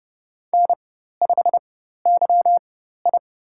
Morse code, heard as a keyed tone, about 30 wpm, spaced as for 14 wpm, 700 Hz. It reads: N5YI
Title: USA Call Signs - 4 Characters - 4x Character Spacing - 30wpm